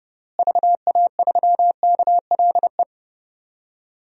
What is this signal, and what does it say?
Morse code, heard as a keyed tone, about 30 wpm, 700 Hz. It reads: VA3KLE
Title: DXCC-100 Call Signs - 6 Characters - Repeated 3x - 30wpm